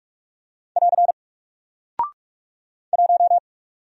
Morse code, heard as a keyed tone, about 44 wpm, 700 Hz. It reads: <AR> 1
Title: Single Character (Letter:Number:Punctuation:Prosign) Alt 45wpm